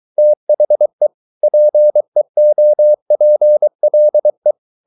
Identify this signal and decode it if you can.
Morse code, heard as a keyed tone, about 23 wpm, 600 Hz. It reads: THE PEOPLE